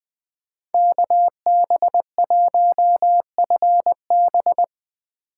Morse code, heard as a keyed tone, about 20 wpm, 700 Hz. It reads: KB1FB
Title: USA Call Signs - 5 Characters - 4x Character Spacing - 20wpm